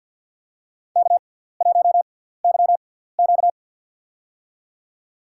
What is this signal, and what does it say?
Morse code, heard as a keyed tone, about 52 wpm, spaced as for 17 wpm, 700 Hz. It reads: K1YQ
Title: USA Call Signs - 4 Characters - 6x Character Spacing - 50wpm